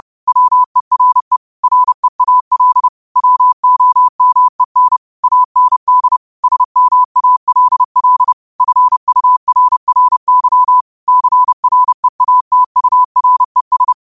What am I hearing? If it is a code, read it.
Morse code, heard as a keyed tone, about 30 wpm, 1000 Hz. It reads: WERE REAL WOMEN AND SMALL FURRY CREATURES
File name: qotdfiles / 30wpm